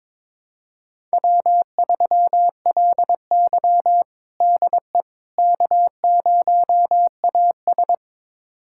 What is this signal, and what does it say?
Morse code, heard as a keyed tone, about 23 wpm, 700 Hz. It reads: W3LY DE K0AS
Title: USA Callsign DE Callsign - 4 Characters - 22wpm